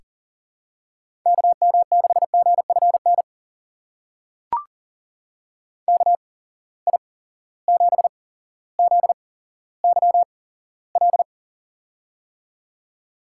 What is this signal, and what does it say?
Morse code, heard as a keyed tone, about 40 wpm, spaced as for 17 wpm, 700 Hz. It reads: KM6GFN KI7ZYL
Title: USA Call Signs - 6 Characters - 8x Character Spacing - 40wpm